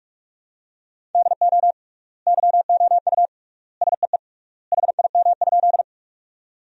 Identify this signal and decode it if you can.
Morse code, heard as a keyed tone, about 41 wpm, 700 Hz. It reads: DO YOU SEE HIM?
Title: 3x Word-Spacing - Sentences from Top 100 Words - 45wpm